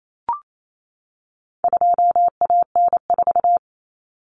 Morse code, heard as a keyed tone, about 28 wpm, 700 Hz. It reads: 2AN4
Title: Common Cut Numbers - 4-Digit 28wpm